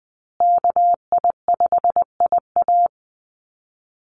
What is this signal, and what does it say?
Morse code, heard as a keyed tone, about 20 wpm, 700 Hz. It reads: KI5IA
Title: USA Call Signs Repeated 3x 20wpm